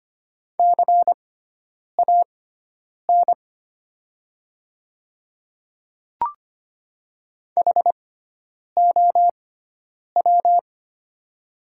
Morse code, heard as a keyed tone, about 25 wpm, spaced as for 9 wpm, 700 Hz. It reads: CAN HOW
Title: 6x Character Spacing - Top 100 Words - 25wpm